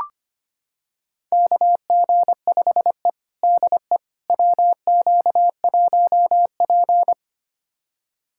Morse code, heard as a keyed tone, about 25 wpm, 700 Hz. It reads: KG5E DE WQ1P
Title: USA Callsign DE Callsign - 4 Characters - 25wpm